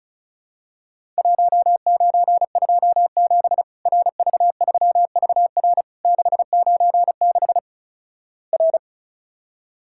Morse code, heard as a keyed tone, about 36 wpm, 700 Hz. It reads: 1927 RV3VR 696 R
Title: Worked All Europe DX Contest - QTC Exchange - 35wpm